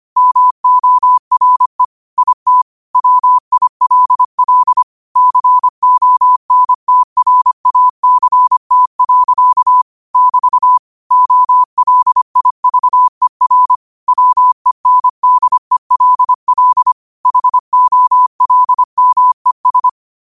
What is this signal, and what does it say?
Morse code, heard as a keyed tone, about 25 wpm, 1000 Hz. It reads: MORE IT WILL CONTRACT. <BT> OLIVER WENDELL HOLMES
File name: qotdfiles / 25wpm